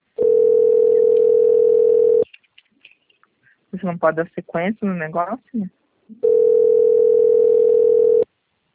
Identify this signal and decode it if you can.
Ringback tone (North America), 440 + 480 Hz, with speech.
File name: rg-1600-unknown- - - .47775